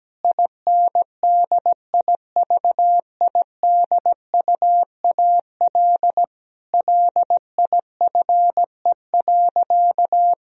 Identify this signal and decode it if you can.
Morse code, heard as a keyed tone, about 17 wpm, 700 Hz. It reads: INDIVIDUAL LIFE.
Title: Sentences from Top 1000 Words - Review Part 4 - 17wpm